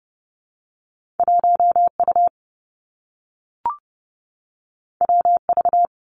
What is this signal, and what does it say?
Morse code, heard as a keyed tone, about 31 wpm, 700 Hz. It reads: 1 U W V
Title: Two Challenging Letter-Number Combinations 30wpm